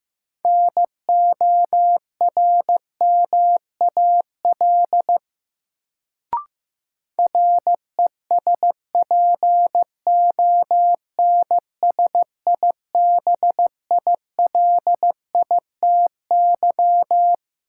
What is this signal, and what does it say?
Morse code, heard as a keyed tone, about 15 wpm, 700 Hz. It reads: NORMAL RESPONSIBILITY
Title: Sentences from Top 1200 Words - New Word Drill Alt - 15wpm